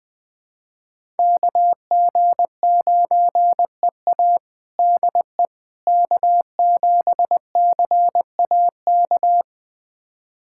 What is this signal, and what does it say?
Morse code, heard as a keyed tone, about 21 wpm, 700 Hz. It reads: KG9EA DE K7CAK
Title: USA Callsign DE Callsign - 5 Characters - 20wpm